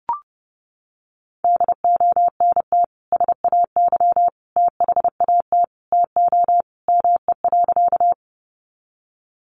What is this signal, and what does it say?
Morse code, heard as a keyed tone, about 30 wpm, 700 Hz. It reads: DONT SAY THAT TO ME.
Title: Sentences from Top 300 Words 30wpm